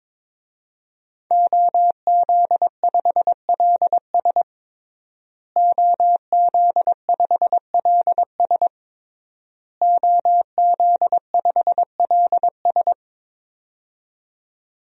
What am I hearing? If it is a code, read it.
Morse code, heard as a keyed tone, about 22 wpm, 700 Hz. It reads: OZ5LS OZ5LS OZ5LS
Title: DXCC-100 Call Signs - 5 Characters - Repeated 3x - 22wpm